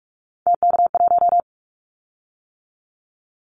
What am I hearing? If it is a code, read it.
Morse code, heard as a keyed tone, about 46 wpm, 700 Hz. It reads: TK1
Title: Callsign Prefixes for Top 100 DXCC - 45wpm